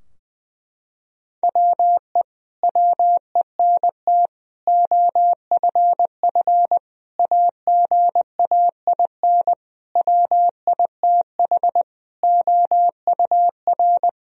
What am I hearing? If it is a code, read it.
Morse code, heard as a keyed tone, about 20 wpm, 700 Hz. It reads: WE WENT OFF AGAIN WITH OUR
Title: Sentences from Top 1000 Words - Review Part 3 - 20wpm